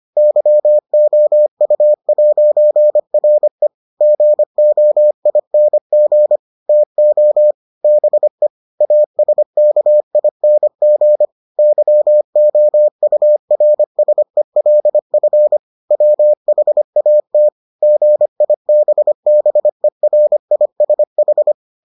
Morse code, heard as a keyed tone, about 25 wpm, 600 Hz. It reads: YOU'RE GOING TO BE ASKING YOURSELF WHAT GIBBERISH